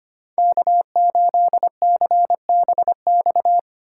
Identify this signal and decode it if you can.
Morse code, heard as a keyed tone, about 25 wpm, 700 Hz. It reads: K8CBX
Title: USA Call Signs - 5 Characters - 6x Character Spacing - 25wpm